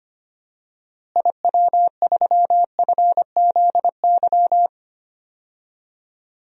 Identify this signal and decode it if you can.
Morse code, heard as a keyed tone, about 25 wpm, 700 Hz. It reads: IW3FZY
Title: DXCC-100 Call Signs - 6 Characters - Repeated 3x - 25wpm